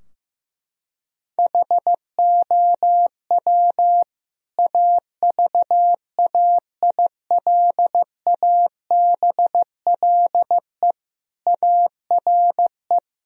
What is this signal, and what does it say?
Morse code, heard as a keyed tone, about 15 wpm, 700 Hz. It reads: HOW AVAILABLE ARE
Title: Sentences from Top 1000 Words - Review Part 3 - 15wpm